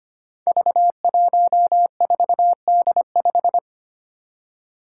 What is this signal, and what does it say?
Morse code, heard as a keyed tone, about 25 wpm, 700 Hz. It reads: V14D5
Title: Five Challenging Letter-Number Combinations 25wpm